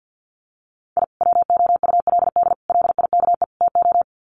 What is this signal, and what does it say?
Morse code, heard as a keyed tone, about 50 wpm, 700 Hz. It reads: I WOULD LIKE TO
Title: Sets of 4 Words 50wpm